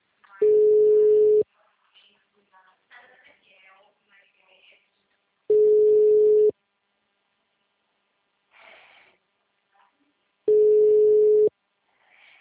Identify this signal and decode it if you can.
Ringback tone (Europe, 425 Hz).